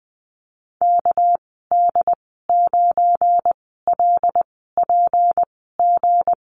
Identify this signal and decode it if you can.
Morse code, heard as a keyed tone, about 20 wpm, spaced as for 14 wpm, 700 Hz. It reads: KD9LPG
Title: 213 Lesson 34g - Number 8 - Call Signs 20wpm